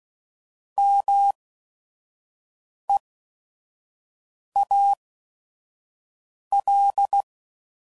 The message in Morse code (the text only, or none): MEAL